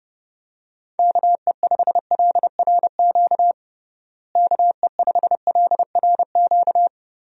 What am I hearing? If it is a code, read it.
Morse code, heard as a keyed tone, about 30 wpm, 700 Hz. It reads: KE5LRQ KE5LRQ
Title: USA Call Signs - 6 Characters - Repeated 2x - 30wpm